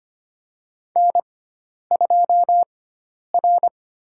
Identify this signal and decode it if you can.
Morse code, heard as a keyed tone, about 25 wpm, 700 Hz. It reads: N 2 R
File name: Lesson 16g - Letter c - Call Signs 25wpm